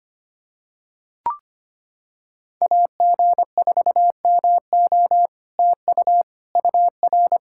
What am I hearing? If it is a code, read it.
Morse code, heard as a keyed tone, about 25 wpm, 700 Hz. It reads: AG4MO TU UR
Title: SOTA QSO Practice 25wpm